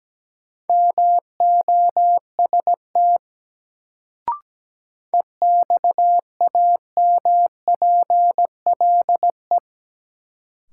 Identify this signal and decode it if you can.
Morse code, heard as a keyed tone, about 17 wpm, 700 Hz. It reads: MOST EXAMPLE
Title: Sets of 2 Words - New Word Drill - 17wpm